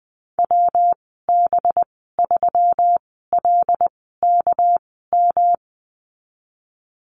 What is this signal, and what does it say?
Morse code, heard as a keyed tone, about 20 wpm, spaced as for 15 wpm, 700 Hz. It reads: WB3LKM
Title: 200 Lesson 32g - Letter k - Call Signs 20wpm